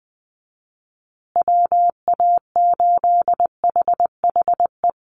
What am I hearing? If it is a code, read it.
Morse code, heard as a keyed tone, about 20 wpm, 700 Hz. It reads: WA8HHE
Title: USA Call Signs Repeated 2x 20wpm